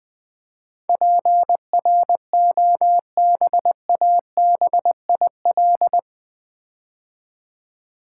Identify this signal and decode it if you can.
Morse code, heard as a keyed tone, about 20 wpm, 700 Hz. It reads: PROBABIL
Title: Build A Word - 4 Letters At A Time - 11-Letter Words - 20wpm